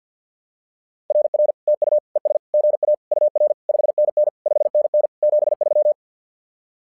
Morse code, heard as a keyed tone, about 50 wpm, 600 Hz. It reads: RR TU ES GA UR 5NN 5NN 73
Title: SOTA QSO Practice 50wpm